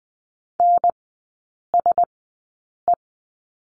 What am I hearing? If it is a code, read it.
Morse code, heard as a keyed tone, about 20 wpm, spaced as for 8 wpm, 700 Hz. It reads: NSE
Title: 035 Lesson 7e - Letter s - Three Characters 20wpm